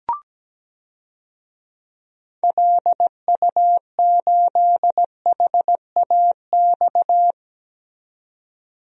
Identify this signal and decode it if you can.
Morse code, heard as a keyed tone, about 17 wpm, 700 Hz. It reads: LU8HAX
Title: DXCC-100 Call Signs - 6 Characters - 17wpm